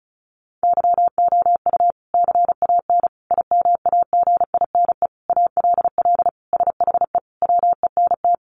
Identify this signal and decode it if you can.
Morse code, heard as a keyed tone, about 34 wpm, 700 Hz. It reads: YOU CAN IMAGINE ALL SHE WENT
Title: Sentences from Top 1400 Words 35wpm